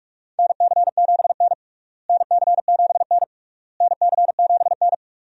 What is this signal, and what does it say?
Morse code, heard as a keyed tone, about 47 wpm, 700 Hz. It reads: NC7N NC7N NC7N
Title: USA Call Signs - 4 Characters - Repeated 3x - 45wpm